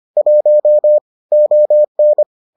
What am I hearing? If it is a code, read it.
Morse code, heard as a keyed tone, about 25 wpm, 600 Hz. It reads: 1 ON